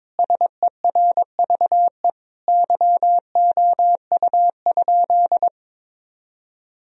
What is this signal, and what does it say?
Morse code, heard as a keyed tone, about 22 wpm, 700 Hz. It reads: SERVE YOU?
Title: Sentences from Top 1000 Words - Review Part 2 - 22wpm